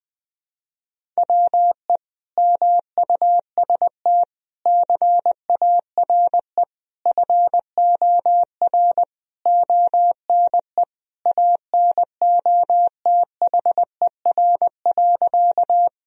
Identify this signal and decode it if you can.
Morse code, heard as a keyed tone, about 20 wpm, 700 Hz. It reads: WE MUST CARE FOR ONE ANOTHER.
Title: Sentences from Top 1000 Words - Review Part 1 - 20wpm